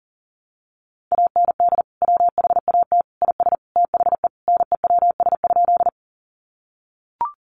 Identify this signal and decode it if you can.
Morse code, heard as a keyed tone, about 40 wpm, 700 Hz. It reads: AND WHAT IS THE NEWS?
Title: Sentences from Top 800 Words 40wpm